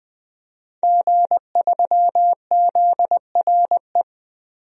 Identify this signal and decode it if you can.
Morse code, heard as a keyed tone, about 21 wpm, 700 Hz. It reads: G3ZRE
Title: DXCC-100 Call Signs - 5 Characters - 20wpm